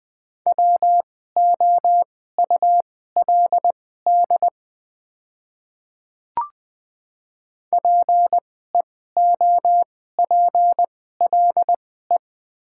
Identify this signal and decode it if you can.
Morse code, heard as a keyed tone, about 20 wpm, spaced as for 14 wpm, 700 Hz. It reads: WOULD PEOPLE
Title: 2x Character Spacing - Top 200 Words - 20wpm